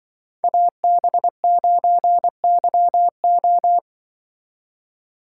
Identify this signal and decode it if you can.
Morse code, heard as a keyed tone, about 24 wpm, 700 Hz. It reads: AB9YO
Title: Speed Racing - USA Call Signs 20wpm